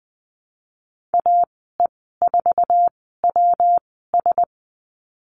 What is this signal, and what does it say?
Morse code, heard as a keyed tone, about 20 wpm, spaced as for 15 wpm, 700 Hz. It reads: AE4WS
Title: 154 Lesson 25g - Letter p - Call Signs 20wpm